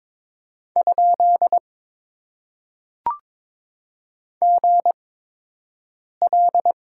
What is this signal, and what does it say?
Morse code, heard as a keyed tone, about 22 wpm, spaced as for 6 wpm, 700 Hz. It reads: ? GL
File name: Top 90 QSO Elements - 8x Character Spacing - 22wpm